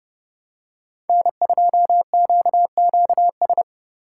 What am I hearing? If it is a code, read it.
Morse code, heard as a keyed tone, about 31 wpm, 690 Hz. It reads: N2QQS
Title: USA Call Signs - 3x Character Spacing - 30wpm